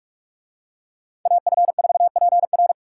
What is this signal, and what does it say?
Morse code, heard as a keyed tone, about 45 wpm, 700 Hz. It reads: AF4PR